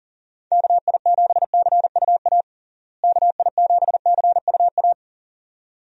Morse code, heard as a keyed tone, about 37 wpm, 700 Hz. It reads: KI7CUA KI7CUA
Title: USA Call Signs - 6 Characters - Repeated 2x - 40wpm